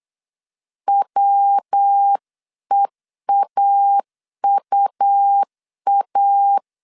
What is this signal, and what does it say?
Morse code, heard as a keyed tone, about 8 wpm, 800 Hz. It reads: WEAUA